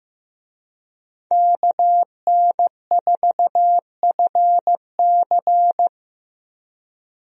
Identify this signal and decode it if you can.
Morse code, heard as a keyed tone, about 15 wpm, 700 Hz. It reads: KN4FC
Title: USA Call Signs - 5 Characters - 15wpm